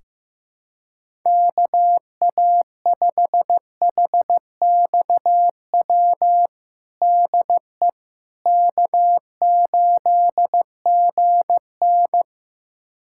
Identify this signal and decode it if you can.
Morse code, heard as a keyed tone, about 15 wpm, 700 Hz. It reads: KA5HXW DE K8GN
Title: USA Callsign DE Callsign 15wpm